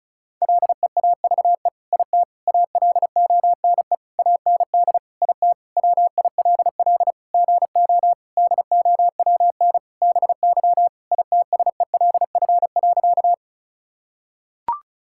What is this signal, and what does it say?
Morse code, heard as a keyed tone, about 36 wpm, 700 Hz. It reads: LEAVE IT ALONE AND IT WILL GO DOWN BY ITSELF.
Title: Sentences from Top 1000 Words - Review Part 5 - 35wpm